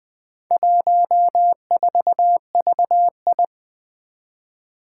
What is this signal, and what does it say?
Morse code, heard as a keyed tone, about 20 wpm, 700 Hz. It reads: 14VI
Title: Four Challenging Letter-Number Combinations 20wpm